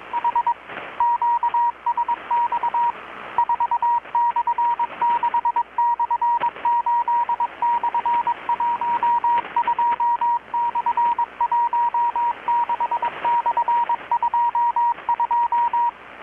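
Morse code, heard as a keyed tone, about 22 wpm, 950 Hz, with radio noise. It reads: H QSX 4/6/8/12/16/22
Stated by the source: Morse from KPH's Night of Nights XX maritime radiotelegraphy event, received in Chiapas, Mexico about 3,800 km away on a Kenwood R-600 with a grounded 10 m long-wire antenna.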